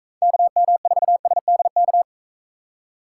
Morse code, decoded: KM4SDK

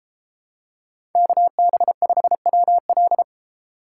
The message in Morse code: KB5WL

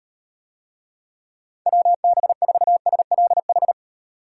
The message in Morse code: WB4SLH

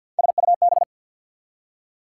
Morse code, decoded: SUB

